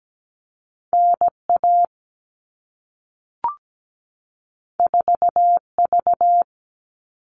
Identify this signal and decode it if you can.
Morse code, heard as a keyed tone, about 17 wpm, 700 Hz. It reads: NA 4V
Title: All Cut Numbers - 2-Digit 17wpm